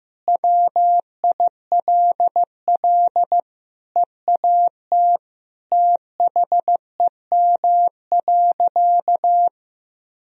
Morse code, heard as a keyed tone, about 15 wpm, 700 Hz. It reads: WILL EAT THEM.
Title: Sentences from Top 1000 Words - Review Part 4 - 15wpm